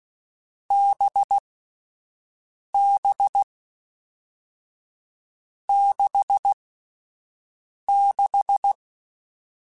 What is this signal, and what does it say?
Morse code, heard as a keyed tone, about 16 wpm, 800 Hz. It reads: B B 6 6
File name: Difficult Combos / Difficult Combos-mp3s / awj1 ndb6 gz7 @10